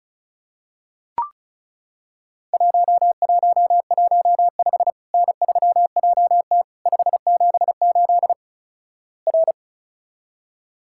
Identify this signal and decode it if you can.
Morse code, heard as a keyed tone, about 36 wpm, 700 Hz. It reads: 1115 N3JT 578 R
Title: Worked All Europe DX Contest - QTC Exchange - 35wpm